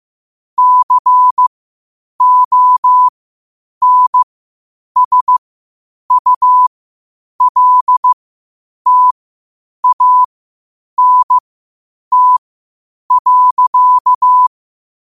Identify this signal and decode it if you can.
Morse code, heard as a keyed tone, about 15 wpm, spaced as for 8 wpm, 1000 Hz. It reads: CONSULTANT.
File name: qotdfiles / 8wpm